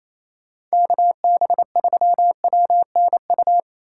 Morse code, heard as a keyed tone, about 36 wpm, 700 Hz. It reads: KB3WNU